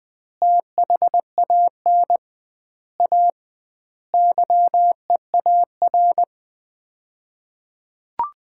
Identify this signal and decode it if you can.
Morse code, heard as a keyed tone, about 20 wpm, 700 Hz. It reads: THAN A YEAR
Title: 2x Word-Spacing - Sets of 5 Words - 20wpm